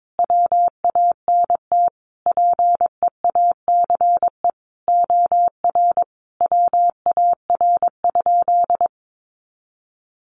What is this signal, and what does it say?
Morse code, heard as a keyed tone, about 22 wpm, 700 Hz. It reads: WANT PEACE OR WAR?
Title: Sentences from Top 1500 Words 22wpm